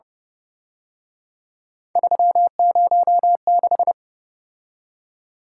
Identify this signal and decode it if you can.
Morse code, heard as a keyed tone, about 29 wpm, 700 Hz. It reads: 306